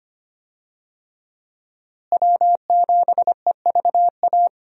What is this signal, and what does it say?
Morse code, heard as a keyed tone, about 25 wpm, 700 Hz. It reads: W7EVA
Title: USA Call Signs - 5 Characters - Repeated 3x - 25wpm